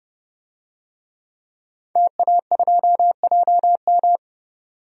Morse code, heard as a keyed tone, about 32 wpm, 700 Hz. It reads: TA2JM
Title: DXCC-100 Call Signs - 5 Characters - 30wpm